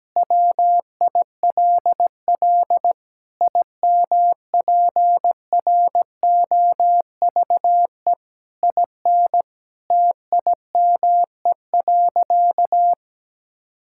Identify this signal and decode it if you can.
Morse code, heard as a keyed tone, about 17 wpm, 700 Hz. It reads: WILL IMPROVE IN TIME.
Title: Sentences from Top 1000 Words - Review Part 5 - 17wpm